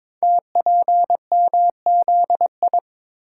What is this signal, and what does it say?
Morse code, heard as a keyed tone, about 23 wpm, 700 Hz. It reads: TPMZI
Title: Code Groups - 5 Letters - 22wpm